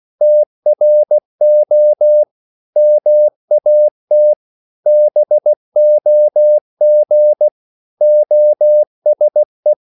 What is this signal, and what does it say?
Morse code, heard as a keyed tone, about 16 wpm, 600 Hz. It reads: TRO MAT BOG OSE
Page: Korte danske ord 16wpm